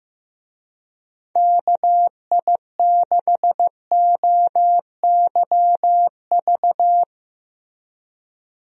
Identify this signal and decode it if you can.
Morse code, heard as a keyed tone, about 15 wpm, 700 Hz. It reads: KI6OYV
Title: USA Call Signs - 6 Characters - Repeated 2x - 15wpm